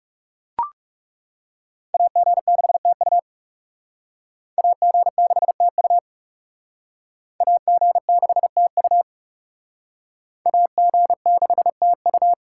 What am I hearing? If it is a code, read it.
Morse code, heard as a keyed tone, about 36 wpm, 700 Hz. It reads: AG6TU AG6TU AG6TU AG6TU
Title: Speed Racing - USA 2x2 Call Signs 30wpm